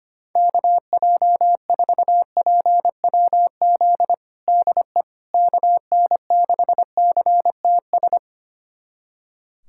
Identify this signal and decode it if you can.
Morse code, heard as a keyed tone, about 25 wpm, 700 Hz. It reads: KJ4PWZ DE KN6CTS
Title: USA Callsign DE Callsign 25wpm